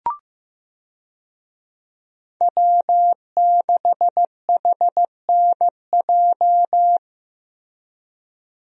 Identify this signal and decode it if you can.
Morse code, heard as a keyed tone, about 15 wpm, 700 Hz. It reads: W6HNJ
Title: USA Call Signs - 5 Characters - Repeated 3x - 15wpm